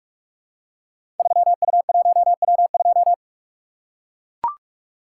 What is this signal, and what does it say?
Morse code, heard as a keyed tone, about 44 wpm, 700 Hz. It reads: 3U1W2
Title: Code Groups - 5 Letter:Numbers - 45wpm